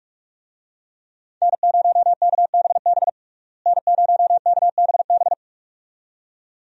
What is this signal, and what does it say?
Morse code, heard as a keyed tone, about 46 wpm, 700 Hz. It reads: N0KBB N0KBB
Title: USA Call Signs - 5 Characters - Repeated 2x - 45wpm